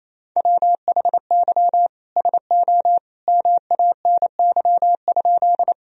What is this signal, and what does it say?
Morse code, heard as a keyed tone, about 28 wpm, 700 Hz. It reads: WHY SO MANY?
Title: Sentences from Top 1000 Words - Review Part 1 - 28wpm